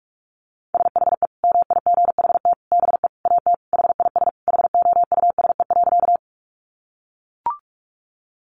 Morse code, heard as a keyed tone, about 44 wpm, 700 Hz. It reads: SHE MIGHT BE AT HIS HOUSE.
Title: Sentences from Top 300 Words 45wpm